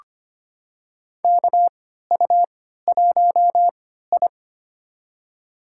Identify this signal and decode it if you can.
Morse code, heard as a keyed tone, about 25 wpm, spaced as for 14 wpm, 700 Hz. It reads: KU1I
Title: USA Call Signs - 4 Characters - 3x Character Spacing - 25wpm